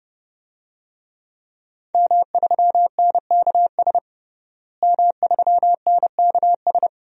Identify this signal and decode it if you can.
Morse code, heard as a keyed tone, about 30 wpm, 700 Hz. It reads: M3NKS M3NKS
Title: DXCC-100 Call Signs - 5 Characters - Repeated 3x - 30wpm